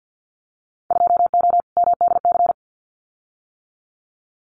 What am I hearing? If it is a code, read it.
Morse code, heard as a keyed tone, about 50 wpm, 700 Hz. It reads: 2O MDC
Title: Winter Field Day Exchanges Encore 50wpm